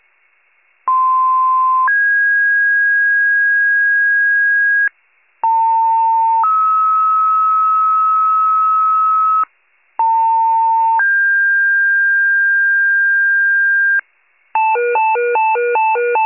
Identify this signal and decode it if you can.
Two-tone page, heard as a one-second tone, then a three-second tone, about 1040 then 1650 Hz, about 900 then 1290 Hz, about 900 then 1600 Hz. It is heard from 0.9 to 14.0 s.